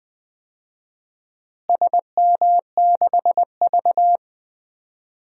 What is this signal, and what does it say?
Morse code, heard as a keyed tone, about 21 wpm, 700 Hz. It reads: SM6V
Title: DXCC-100 Call Signs - 4 Characters - Repeated 3x - 20wpm